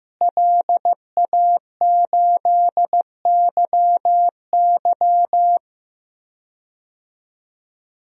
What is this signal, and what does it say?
Morse code, heard as a keyed tone, about 15 wpm, 700 Hz. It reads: LA8YY
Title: DXCC-100 Call Signs - 5 Characters - Repeated 3x - 15wpm